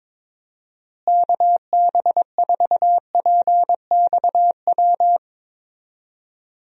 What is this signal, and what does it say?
Morse code, heard as a keyed tone, about 22 wpm, 700 Hz. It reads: KB4PXW
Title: USA Call Signs - 6 Characters - Repeated 2x - 22wpm